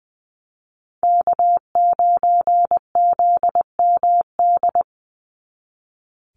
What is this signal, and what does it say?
Morse code, heard as a keyed tone, about 19 wpm, 700 Hz. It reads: K9ZMD
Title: NA SOTA Chaser Callsigns 20wpm